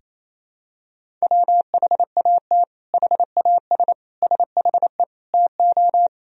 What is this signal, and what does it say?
Morse code, heard as a keyed tone, about 28 wpm, 700 Hz. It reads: WHAT HAS SHE TO
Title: Sentences from Top 100 Words and Abbreviations 28wpm